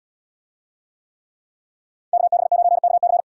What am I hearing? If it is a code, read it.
Morse code, heard as a keyed tone, about 70 wpm, 690 Hz. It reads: KB9KC